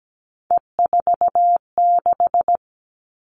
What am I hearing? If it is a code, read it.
Morse code, heard as a keyed tone, about 17 wpm, 700 Hz. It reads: E46
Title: All Cut Numbers - 3-Digit 17wpm